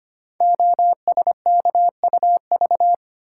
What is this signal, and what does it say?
Morse code, heard as a keyed tone, about 25 wpm, 700 Hz. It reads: OSKUV
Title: Code Groups - 5 Letters - 25wpm